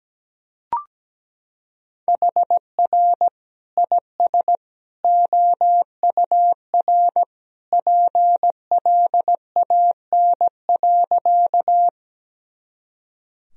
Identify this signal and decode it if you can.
Morse code, heard as a keyed tone, about 17 wpm, 700 Hz. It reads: HR IS OUR PLAN.
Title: Sentences from Top 500 Words and Abbreviations 17wpm